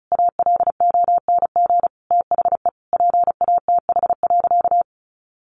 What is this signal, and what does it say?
Morse code, heard as a keyed tone, about 34 wpm, 700 Hz. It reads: ALONG THE PATH.
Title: A Sound Of Thunder 35wpm